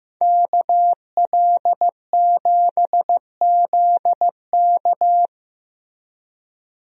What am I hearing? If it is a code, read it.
Morse code, heard as a keyed tone, about 15 wpm, 700 Hz. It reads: KL7ZK
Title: USA Call Signs - 5 Characters - Repeated 3x - 15wpm